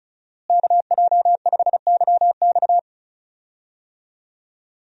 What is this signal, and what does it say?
Morse code, heard as a keyed tone, about 34 wpm, 700 Hz. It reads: KJ5YX
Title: USA Call Signs - 5 Characters - Repeated 2x - 35wpm